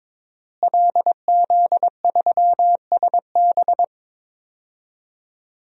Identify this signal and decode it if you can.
Morse code, heard as a keyed tone, about 21 wpm, 700 Hz. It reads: LZ3SB
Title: DXCC-100 Call Signs - 5 Characters - Repeated 3x - 22wpm